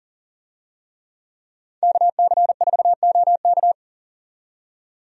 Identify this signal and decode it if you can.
Morse code, heard as a keyed tone, about 40 wpm, 700 Hz. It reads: KC4OK
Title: USA Call Signs - 5 Characters - 40wpm